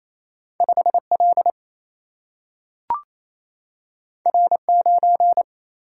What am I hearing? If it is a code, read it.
Morse code, heard as a keyed tone, about 28 wpm, 700 Hz. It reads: 5 L R 9
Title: Single-Letter Single-Number 28wpm